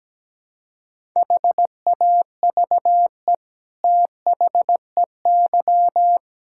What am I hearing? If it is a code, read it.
Morse code, heard as a keyed tone, about 17 wpm, 700 Hz. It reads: HAVE THEY
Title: Sentences from Top 1000 Words - Review Part 6 - 17wpm